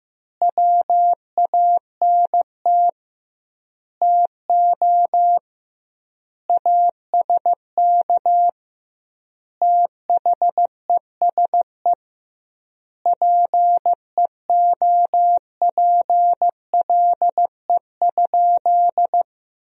Morse code, heard as a keyed tone, about 15 wpm, 700 Hz. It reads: WANT TO ASK THESE PEOPLE?
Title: 2x Word-Spacing - Sentences from Top 200 Words - 15wpm